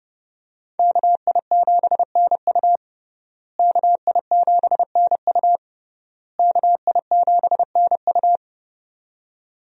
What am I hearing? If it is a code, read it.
Morse code, heard as a keyed tone, about 30 wpm, 700 Hz. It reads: KI7NU KI7NU KI7NU
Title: USA Call Signs - 5 Characters - Repeated 3x - 30wpm